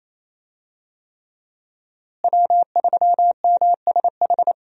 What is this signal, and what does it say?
Morse code, heard as a keyed tone, about 28 wpm, 700 Hz. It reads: W3MSH
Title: USA Call Signs - 5 Characters - Repeated 3x - 28wpm